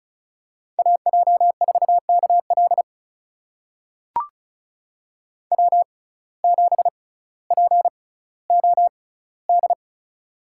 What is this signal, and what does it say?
Morse code, heard as a keyed tone, about 36 wpm, spaced as for 18 wpm, 700 Hz. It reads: AJ4KL W7POD
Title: USA Call Signs - 5 Characters - 6x Character Spacing - 35wpm